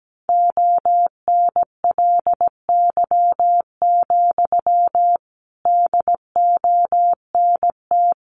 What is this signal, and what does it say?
Morse code, heard as a keyed tone, about 17 wpm, 700 Hz. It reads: ONLY, DONT
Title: Sentences from Top 300 Words 17wpm